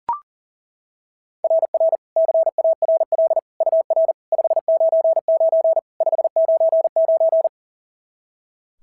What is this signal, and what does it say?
Morse code, heard as a keyed tone, about 37 wpm, 690 Hz. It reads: RR CARL UR 599 599
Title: SOTA QSO Practice 40wpm